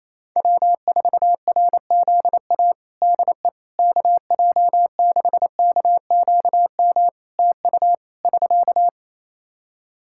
Morse code, heard as a keyed tone, about 28 wpm, 700 Hz. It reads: W4RZA DE KJ6KQM TU <SK>
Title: Practice QSO - Exchange 4 - Ending - 28wpm